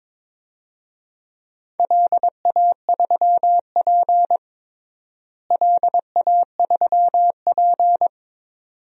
Morse code, heard as a keyed tone, about 22 wpm, 700 Hz. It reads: LA3P LA3P
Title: DXCC-100 Call Signs - 4 Characters - Repeated 2x - 22wpm